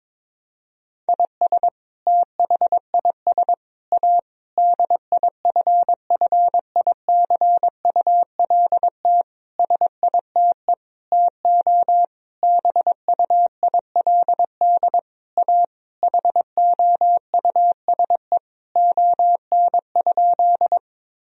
Morse code, heard as a keyed tone, about 22 wpm, 700 Hz. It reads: IS THIS A DIFFICULT SITE TO BUILD A HOUSE ON?
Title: Sentences from Top 1000 Words - Review Part 4 - 22wpm